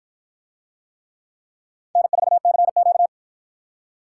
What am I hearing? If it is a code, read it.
Morse code, heard as a keyed tone, about 48 wpm, 700 Hz. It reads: N4CQ